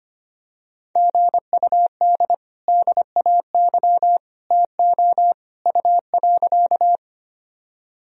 Morse code, heard as a keyed tone, about 25 wpm, 700 Hz. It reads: GUD DAY TO U.
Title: Sentences from Top 200 Words and Abbreviations 25wpm